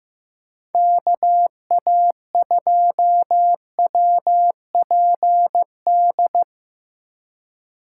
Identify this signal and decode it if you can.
Morse code, heard as a keyed tone, about 15 wpm, 700 Hz. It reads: KA2WPD
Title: USA Call Signs - 2x Character Spacing - 15wpm